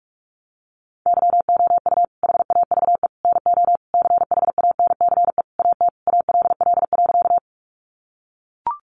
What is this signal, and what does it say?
Morse code, heard as a keyed tone, about 44 wpm, 700 Hz. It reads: YOU HAVE NO CHANCE AT ALL.
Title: Sentences from Top 800 Words 45wpm